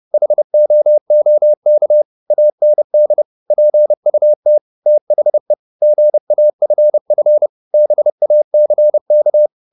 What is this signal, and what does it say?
Morse code, heard as a keyed tone, about 30 wpm, 600 Hz. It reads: HOOK AND PUT THE GAFF BACK
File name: the old man and the sea / the old man and the sea 1 30wpm